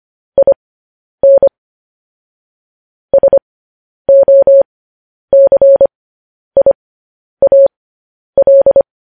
Morse code, heard as a keyed tone, about 25 wpm, spaced as for 10 wpm, 550 Hz. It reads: IN SOCIAL